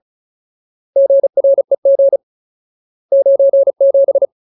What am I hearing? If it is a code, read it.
Morse code, heard as a keyed tone, about 34 wpm, 550 Hz. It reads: GREG 97
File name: Ultimate Word List / 35WPM Ultimate